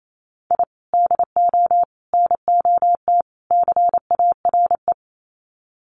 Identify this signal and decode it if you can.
Morse code, heard as a keyed tone, about 28 wpm, 700 Hz. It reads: I DO NOT CARE
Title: Sets of 4 Words 28wpm